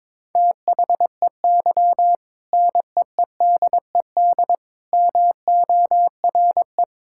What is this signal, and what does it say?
Morse code, heard as a keyed tone, about 22 wpm, 700 Hz. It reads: THEY NEEDED MORE
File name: practice / 22wpm / Sentences with 2-Word Phrases - 22wpm